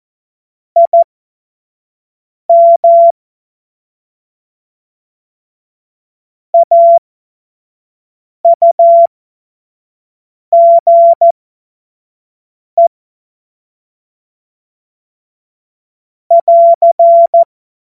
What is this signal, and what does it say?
Morse code, heard as a keyed tone, about 14 wpm, spaced as for 5 wpm, 680 Hz. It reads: IM AUGE <AR>
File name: Morsekurs / Witze / Witz 23 April 2024 25 F70